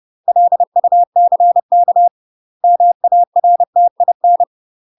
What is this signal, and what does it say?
Morse code, heard as a keyed tone, about 30 wpm, 700 Hz. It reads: LUCK MARTIN
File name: UK MASTER QSOs TRAINING / 30WPM / UK MASTER QSOs 0042